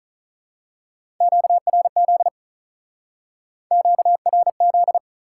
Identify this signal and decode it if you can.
Morse code, heard as a keyed tone, about 41 wpm, 700 Hz. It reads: QRZ QRZ
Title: Speed-Racing Top 180 QSO Elements 35wpm